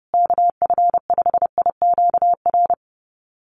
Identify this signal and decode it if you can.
Morse code, heard as a keyed tone, about 30 wpm, 700 Hz. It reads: KF5IQR
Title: USA Call Signs Repeated 3x 30wpm